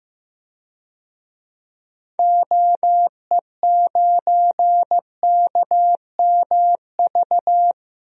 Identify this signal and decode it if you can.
Morse code, heard as a keyed tone, about 15 wpm, 700 Hz. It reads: OE9KMV
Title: DXCC-100 Call Signs - 6 Characters - Repeated 3x - 15wpm